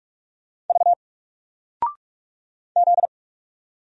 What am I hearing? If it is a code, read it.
Morse code, heard as a keyed tone, about 45 wpm, 700 Hz. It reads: V Z
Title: Single Character (Letter:Number:Punctuation:Prosign) Alt 45wpm